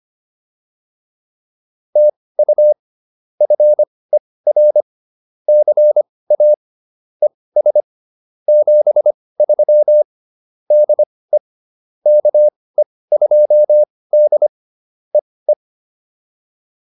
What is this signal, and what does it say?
Morse code, heard as a keyed tone, about 25 wpm, spaced as for 18 wpm, 600 Hz. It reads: TU FER CA ES 73 DE KE2D EE
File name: POTA106 18